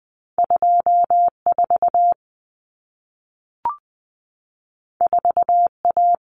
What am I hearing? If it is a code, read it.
Morse code, heard as a keyed tone, about 20 wpm, 700 Hz. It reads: 24 4A
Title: Common Cut Numbers - 2-Digit 20wpm